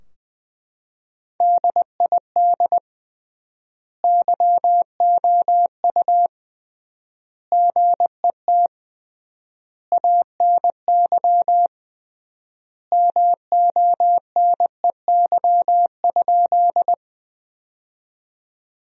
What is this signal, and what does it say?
Morse code, heard as a keyed tone, about 20 wpm, 700 Hz. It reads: DID YOU GET ANY MONEY?
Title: 3x Word-Spacing - Sentences from Top 400 Words - 20wpm